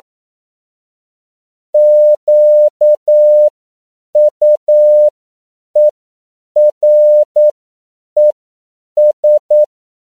Morse code, heard as a keyed tone, about 9 wpm, spaced as for 7 wpm, 600 Hz. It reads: QUERES